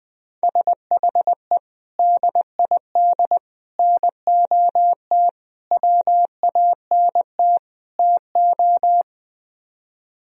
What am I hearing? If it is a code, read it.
Morse code, heard as a keyed tone, about 19 wpm, 700 Hz. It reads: SHE DID NOT WANT TO
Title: Speed-Racing - Sets of 5 Words 17wpm